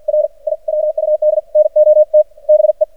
Morse code, heard as a keyed tone, about 49 wpm, 610 Hz. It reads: R IWUD NOT BE
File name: archived / CRSnet